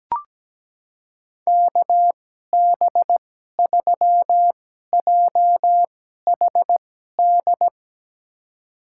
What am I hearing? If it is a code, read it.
Morse code, heard as a keyed tone, about 17 wpm, spaced as for 12 wpm, 700 Hz. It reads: KB3JHD
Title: USA Call Signs - 6 Characters - 2x Character Spacing - 17wpm